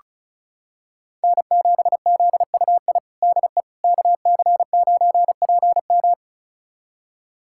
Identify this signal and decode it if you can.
Morse code, heard as a keyed tone, about 36 wpm, 700 Hz. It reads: N7ZUI DE KC9PM
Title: USA Callsign DE Callsign - 5 Characters - 35wpm